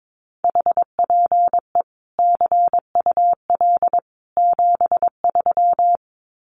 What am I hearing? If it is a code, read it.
Morse code, heard as a keyed tone, about 22 wpm, 700 Hz. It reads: HPE CUL 73
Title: Common QSO Phrases 22wpm